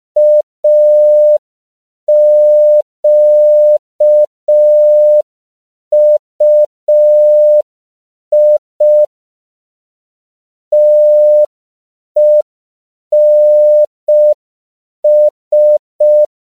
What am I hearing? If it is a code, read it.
Morse code, heard as a keyed tone, about 5 wpm, 600 Hz. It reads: AQUI TENS
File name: CW / 5PPM